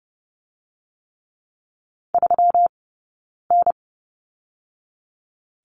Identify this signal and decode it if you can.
Morse code, heard as a keyed tone, about 30 wpm, 700 Hz. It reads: 3 N
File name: Lesson 20d - Number 3 - Two Characters 30wpm